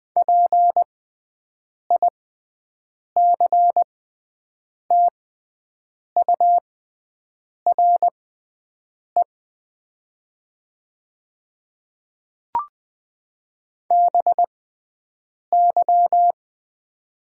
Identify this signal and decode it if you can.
Morse code, heard as a keyed tone, about 20 wpm, spaced as for 7 wpm, 700 Hz. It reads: PICTURE BY
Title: 6x Character Spacing - Top 200 Words - 20wpm